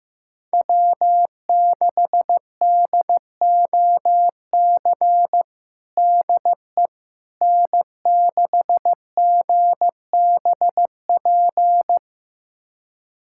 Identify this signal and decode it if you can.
Morse code, heard as a keyed tone, about 15 wpm, 700 Hz. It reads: W6DOC DE N6GBP
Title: USA Callsign DE Callsign - 5 Characters - 15wpm